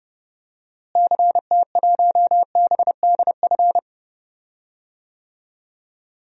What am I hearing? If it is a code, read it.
Morse code, heard as a keyed tone, about 32 wpm, 700 Hz. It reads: CT1BDF
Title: DXCC-100 Call Signs - 6 Characters - Repeated 3x - 30wpm